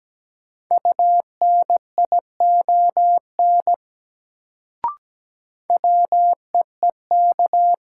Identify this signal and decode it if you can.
Morse code, heard as a keyed tone, about 17 wpm, 700 Hz. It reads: UNION WEEK
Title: Sets of 5 Words Encore - New Word Drill - 17wpm